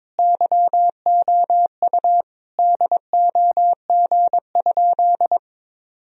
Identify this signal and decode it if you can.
Morse code, heard as a keyed tone, about 22 wpm, 700 Hz. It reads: YOU DOG?
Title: Sentences from Top 1000 Words - Review Part 2 - 22wpm